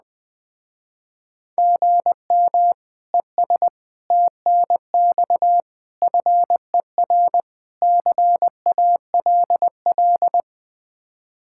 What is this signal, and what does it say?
Morse code, heard as a keyed tone, about 20 wpm, 700 Hz. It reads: GM ES TNX FER CALL